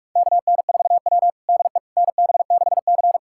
Morse code, heard as a keyed tone, about 46 wpm, 700 Hz. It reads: KN4W DE NB6C
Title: USA Callsign DE Callsign - 4 Characters - 45wpm